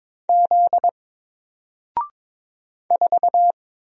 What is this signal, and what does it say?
Morse code, heard as a keyed tone, about 22 wpm, 700 Hz. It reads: Z 4
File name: Single-Letter-Number-22wpm